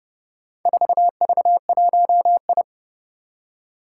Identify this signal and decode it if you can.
Morse code, heard as a keyed tone, about 30 wpm, 690 Hz. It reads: 4V1I
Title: Four Challenging Letter-Number Combinations 30wpm